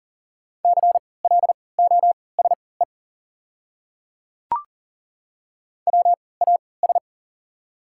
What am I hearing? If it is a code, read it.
Morse code, heard as a keyed tone, about 40 wpm, spaced as for 23 wpm, 700 Hz. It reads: CLOSE WAS
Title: 3x Character Spacing - Top 300 Words - 40wpm